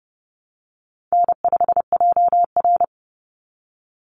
Morse code, heard as a keyed tone, about 30 wpm, 700 Hz. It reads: N5JR
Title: Easy Call Signs Repeated 3x - 30wpm